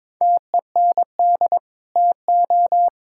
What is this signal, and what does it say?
Morse code, heard as a keyed tone, about 22 wpm, 700 Hz. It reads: TEND TO
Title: Sentences from Top 1000 Words - Review Part 5 - 22wpm